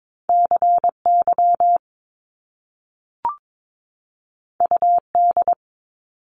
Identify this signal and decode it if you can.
Morse code, heard as a keyed tone, about 22 wpm, 700 Hz. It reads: CY UD
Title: Top Two Letter Combinations in Words 22wpm